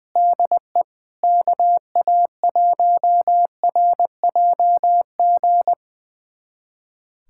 Morse code, heard as a keyed tone, about 19 wpm, 700 Hz. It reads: DE KA1RJG
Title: USA Callsign DE Callsign 20wpm